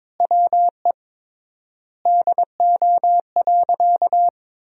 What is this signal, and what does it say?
Morse code, heard as a keyed tone, about 22 wpm, 700 Hz. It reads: WE DO.
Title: 3x Word-Spacing - Sentences from Top 100 Words - 22wpm